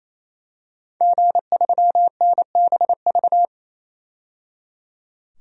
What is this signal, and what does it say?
Morse code, heard as a keyed tone, about 27 wpm, 700 Hz. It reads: G3NBV
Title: DXCC-100 Call Signs - 5 Characters - 28wpm